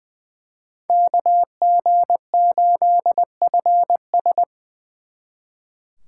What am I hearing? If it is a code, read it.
Morse code, heard as a keyed tone, about 24 wpm, 700 Hz. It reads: KG8FS
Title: Speed Racing - USA 2x2 Call Signs 20wpm